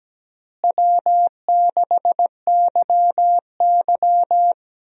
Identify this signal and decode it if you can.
Morse code, heard as a keyed tone, about 17 wpm, 700 Hz. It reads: W6YY
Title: USA Call Signs - 4 Characters - Repeated 2x - 17wpm